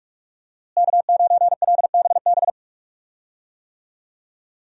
Morse code, heard as a keyed tone, about 45 wpm, 700 Hz. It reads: K9LBB
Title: USA Call Signs - 5 Characters - Repeated 2x - 45wpm